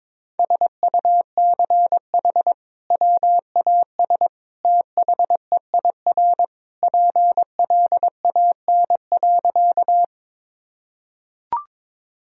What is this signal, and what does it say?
Morse code, heard as a keyed tone, about 22 wpm, 700 Hz. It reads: SUCH WAS THEIR PLAN.
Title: Sentences from Top 1000 Words - Review Part 2 - 22wpm